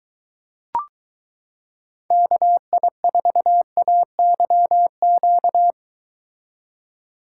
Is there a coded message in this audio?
Yes: Morse code from 2.1 to 5.7 s, heard as a keyed tone, about 22 wpm, 700 Hz. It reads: KI4AYQ